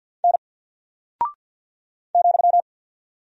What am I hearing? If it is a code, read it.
Morse code, heard as a keyed tone, about 48 wpm, 700 Hz. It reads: N ,
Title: Single Character (Letter:Number:Punctuation:Prosign) Alt 50wpm